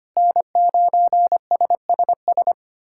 Morse code, heard as a keyed tone, about 25 wpm, 700 Hz. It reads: N9SSS
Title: USA Callsign DE Callsign 25wpm